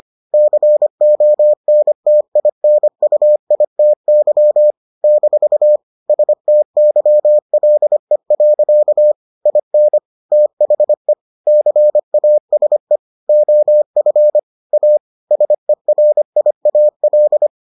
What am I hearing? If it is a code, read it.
Morse code, heard as a keyed tone, about 25 wpm, 600 Hz. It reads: CONTINUITY - STYLE. IN THE CASE OF A SERIAL